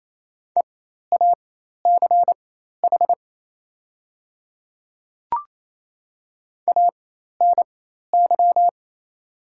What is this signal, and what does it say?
Morse code, heard as a keyed tone, about 27 wpm, spaced as for 13 wpm, 690 Hz. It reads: EACH ANY
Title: 4x Character Spacing - Top 200 Words - 28wpm